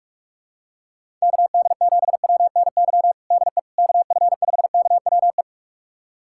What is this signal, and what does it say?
Morse code, heard as a keyed tone, about 41 wpm, 700 Hz. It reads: KD7WNY DE KF5KWE
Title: USA Callsign DE Callsign - 6 Characters - 45wpm